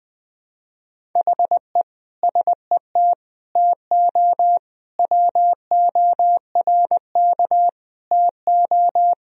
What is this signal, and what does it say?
Morse code, heard as a keyed tone, about 20 wpm, 700 Hz. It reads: HE SET TO WORK TO
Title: Sentences from Top 1000 Words - Review Part 1 - 20wpm